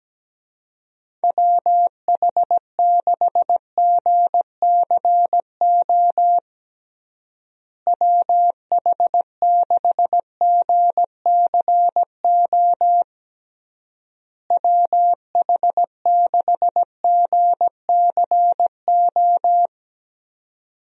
Morse code, heard as a keyed tone, about 17 wpm, 700 Hz. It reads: WH6GCO WH6GCO WH6GCO
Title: USA Call Signs - 6 Characters - Repeated 3x - 17wpm